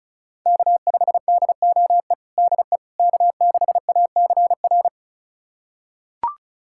Morse code, heard as a keyed tone, about 36 wpm, 700 Hz. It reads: K5DOE DE K6ACR
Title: USA Callsign DE Callsign - 5 Characters - 35wpm